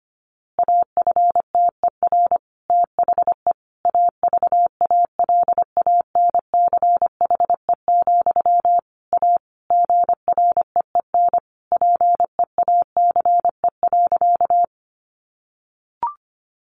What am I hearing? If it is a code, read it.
Morse code, heard as a keyed tone, about 25 wpm, 700 Hz. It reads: AFTER THE AVALANCHE, A GREEN PEACE.
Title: A Sound Of Thunder 25wpm